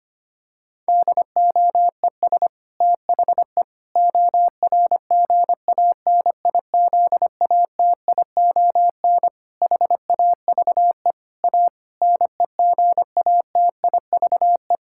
Morse code, heard as a keyed tone, about 25 wpm, 700 Hz. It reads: DOES THE ORGANIZATION HAVE A NEGATIVE
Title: Sentences from Top 1000 Words - Review Part 6 - 25wpm